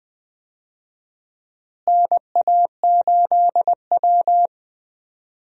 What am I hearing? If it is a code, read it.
Morse code, heard as a keyed tone, about 20 wpm, 700 Hz. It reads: NA8W
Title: USA Call Signs - 4 Characters - Repeated 3x - 20wpm